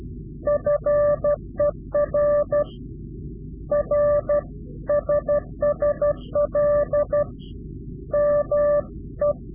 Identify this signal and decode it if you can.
Morse code, heard as a keyed tone, about 12 wpm, 590 Hz. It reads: FER R SSL ME
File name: CW-QSO / Directional Call / Directional QSO